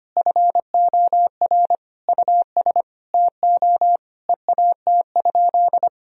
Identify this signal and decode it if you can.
Morse code, heard as a keyed tone, about 25 wpm, 700 Hz. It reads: FOR US TO EAT?
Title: Sentences from Top 1000 Words - Review Part 3 - 25wpm